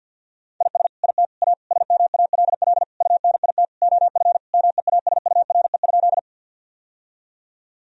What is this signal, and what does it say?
Morse code, heard as a keyed tone, about 50 wpm, 700 Hz. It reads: IS IT A SMALL UNIT OF MEASURE?
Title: Sentences from Top 1000 Words - Review Part 2 - 50wpm